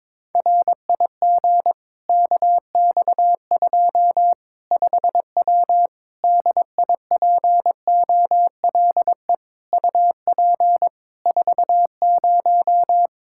Morse code, heard as a keyed tone, about 22 wpm, 700 Hz. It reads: RIG KX2 5W DIPOLE UP 40
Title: Practice QSO - Exchange 2 - Info - 22wpm